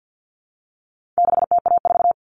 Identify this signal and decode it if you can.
Morse code, heard as a keyed tone, about 48 wpm, 700 Hz. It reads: 6TA4
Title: All Cut Numbers - 4-Digit 50wpm